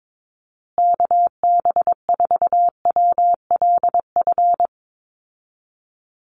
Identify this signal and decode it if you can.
Morse code, heard as a keyed tone, about 22 wpm, 700 Hz. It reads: KB4WLF
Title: Call Signs Repeated 3x - 22wpm